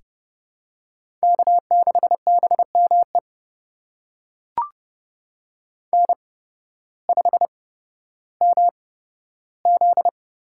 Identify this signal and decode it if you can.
Morse code, heard as a keyed tone, about 30 wpm, spaced as for 14 wpm, 700 Hz. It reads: K6BME N5MZ
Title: USA Call Signs - 5 Characters - 8x Character Spacing - 30wpm